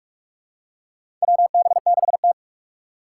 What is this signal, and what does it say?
Morse code, heard as a keyed tone, about 44 wpm, 700 Hz. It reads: WB6T
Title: USA Call Signs - 4 Characters - 4x Character Spacing - 45wpm